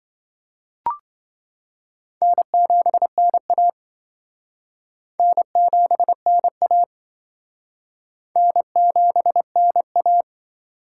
Morse code, heard as a keyed tone, about 25 wpm, 700 Hz. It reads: N7NA N7NA N7NA